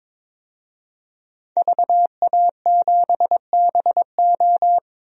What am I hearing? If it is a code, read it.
Morse code, heard as a keyed tone, about 22 wpm, 700 Hz. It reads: VA7BO
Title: DXCC-100 Call Signs - 5 Characters - 22wpm